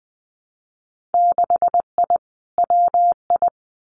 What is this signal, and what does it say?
Morse code, heard as a keyed tone, about 20 wpm, 700 Hz. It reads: 6I WI
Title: Winter Field Day Exchanges Encore 20wpm